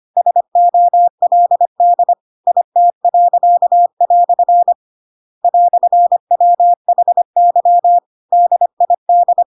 Morse code, heard as a keyed tone, about 25 wpm, 700 Hz. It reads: SOLD IT." "WHY DID
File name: Pinocchio / Pinocchio ch8 25 25 ik1hge